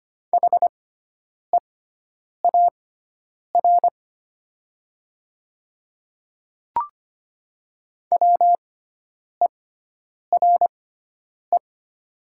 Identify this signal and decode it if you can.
Morse code, heard as a keyed tone, about 25 wpm, spaced as for 9 wpm, 700 Hz. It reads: HEAR WERE
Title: 6x Character Spacing - Top 300 Words - 25wpm